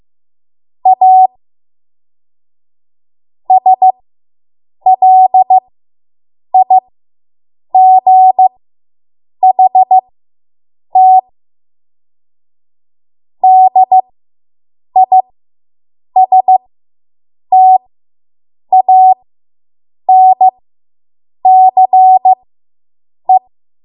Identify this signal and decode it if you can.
Morse code, heard as a keyed tone, about 15 wpm, spaced as for 7 wpm, 750 Hz. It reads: A SLIGHT DISTANCE